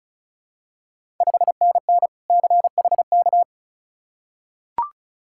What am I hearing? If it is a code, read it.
Morse code, heard as a keyed tone, about 36 wpm, 700 Hz. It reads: 5NN CHK
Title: Ontario QSO Party Exchange 35wpm